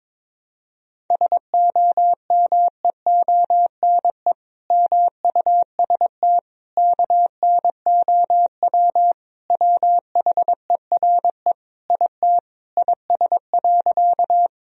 Morse code, heard as a keyed tone, about 22 wpm, 700 Hz. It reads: SOMEONE MUST KNOW WHERE IT IS.
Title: Sentences from Top 1000 Words - Review Part 4 - 22wpm